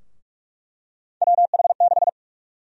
Morse code, heard as a keyed tone, about 46 wpm, 700 Hz. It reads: WH6
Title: USA Call Sign Prefixes - 45wpm